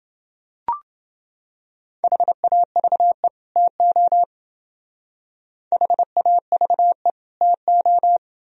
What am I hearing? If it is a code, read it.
Morse code, heard as a keyed tone, about 28 wpm, 700 Hz. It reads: HAVE TO HAVE TO